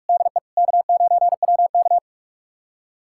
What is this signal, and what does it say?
Morse code, heard as a keyed tone, about 42 wpm, 700 Hz. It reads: DE K9WK
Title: USA Callsign DE Callsign - 4 Characters - 45wpm